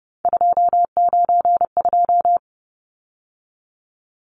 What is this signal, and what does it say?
Morse code, heard as a keyed tone, about 30 wpm, 700 Hz. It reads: 292
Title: Numbers - 3-Digit 30wpm